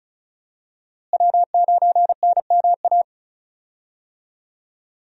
Morse code, heard as a keyed tone, about 36 wpm, 700 Hz. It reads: W9NMA
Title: USA Call Signs - 5 Characters - Repeated 2x - 35wpm